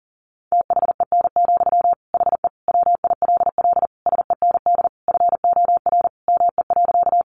Morse code, heard as a keyed tone, about 40 wpm, 700 Hz. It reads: THEN, HE WILL SEND FOR ME.
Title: Sentences from Top 1500 Words 40wpm